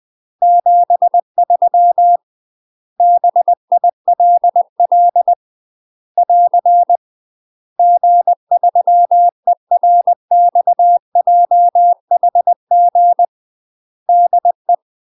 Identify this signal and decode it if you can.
Morse code, heard as a keyed tone, about 20 wpm, 700 Hz. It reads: 73 BILL <AR> G3ERXJHG DE
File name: UK MASTER QSOs TRAINING / 20WPM / UK MASTER QSOs 0024